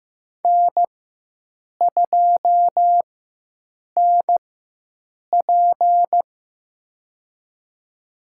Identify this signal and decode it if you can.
Morse code, heard as a keyed tone, about 15 wpm, spaced as for 7 wpm, 700 Hz. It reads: N2NP
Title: USA Call Signs - 4 Characters - 4x Character Spacing - 15wpm